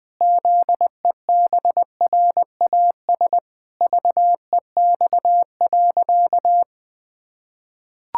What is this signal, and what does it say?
Morse code, heard as a keyed tone, about 21 wpm, 700 Hz. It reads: ZEBRAS VEX.
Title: Pangram Challenge v2 - 20wpm